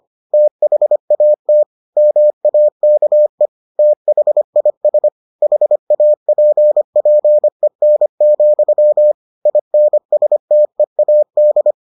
Morse code, heard as a keyed tone, about 25 wpm, 600 Hz. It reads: THAT MAKE THIS HAPPEN, INSTEAD